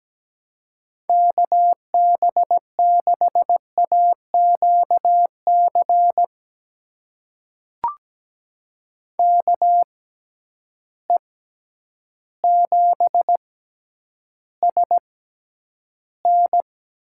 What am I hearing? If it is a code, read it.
Morse code, heard as a keyed tone, about 17 wpm, spaced as for 9 wpm, 700 Hz. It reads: KB6AQC KE7SN
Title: USA Call Signs - 6 Characters - 6x Character Spacing - 17wpm